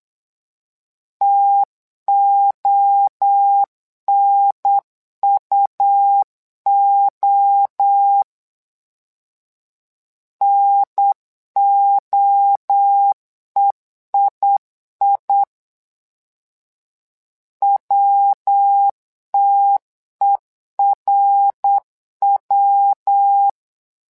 Morse code, heard as a keyed tone, about 8 wpm, 800 Hz. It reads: TONUO NOEII WTERW
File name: LEZIONI / 3 MWRO / 3 MWRO 8wpm 2606